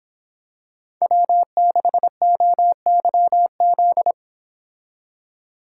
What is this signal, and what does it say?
Morse code, heard as a keyed tone, about 26 wpm, 700 Hz. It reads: W6OYZ